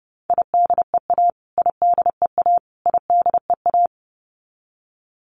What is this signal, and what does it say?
Morse code, heard as a keyed tone, about 30 wpm, 700 Hz. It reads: IDEA IDEA IDEA
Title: Top 500 Words Repeated 30wpm